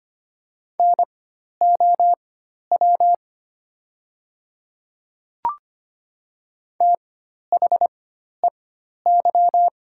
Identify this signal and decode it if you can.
Morse code, heard as a keyed tone, about 25 wpm, spaced as for 12 wpm, 700 Hz. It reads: NOW THEY
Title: 4x Character Spacing - Top 100 Words - 25wpm